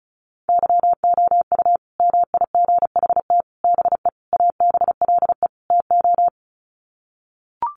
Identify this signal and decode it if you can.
Morse code, heard as a keyed tone, about 34 wpm, 700 Hz. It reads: YOU MIGHT BE ABLE TO
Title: Sets of 5 Words Encore2 35wpm